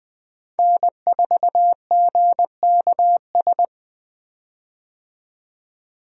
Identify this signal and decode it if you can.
Morse code, heard as a keyed tone, about 20 wpm, 700 Hz. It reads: N4GKS
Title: USA Call Signs - 5 Characters - 20wpm